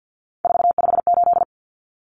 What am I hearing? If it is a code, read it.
Morse code, heard as a keyed tone, about 50 wpm, 700 Hz. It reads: 458
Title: Numbers - 3-Digit 50wpm